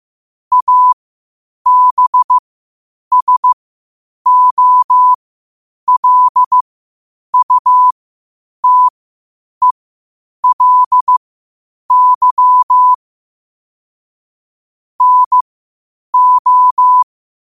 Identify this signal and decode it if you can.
Morse code, heard as a keyed tone, about 15 wpm, spaced as for 8 wpm, 1000 Hz. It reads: ABSOLUTELY NO
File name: qotdfiles / 8wpm